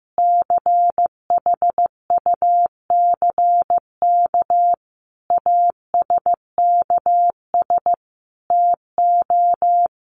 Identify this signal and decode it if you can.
Morse code, heard as a keyed tone, about 15 wpm, 700 Hz. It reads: CHUCK ASKS TO
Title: Additional Easy Phrases 15wpm